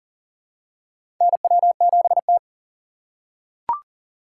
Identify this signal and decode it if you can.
Morse code, heard as a keyed tone, about 40 wpm, 700 Hz. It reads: NW7T
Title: USA Call Signs - 2x Character Spacing - 40wpm